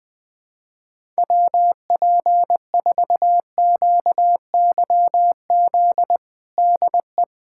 Morse code, heard as a keyed tone, about 20 wpm, 700 Hz. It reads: WP4QYZ DE
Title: USA Callsign DE Callsign - 6 Characters - 20wpm